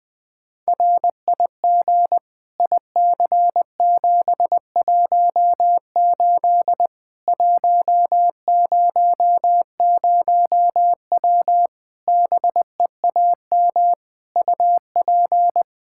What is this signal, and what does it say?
Morse code, heard as a keyed tone, about 20 wpm, 700 Hz. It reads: RIG IC718 100W BEAM UP
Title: Practice QSO - Exchange 2 - Info - 20wpm